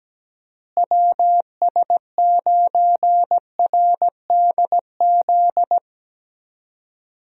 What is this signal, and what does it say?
Morse code, heard as a keyed tone, about 17 wpm, 700 Hz. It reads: WS9RDZ
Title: USA Call Signs - 6 Characters - Repeated 2x - 17wpm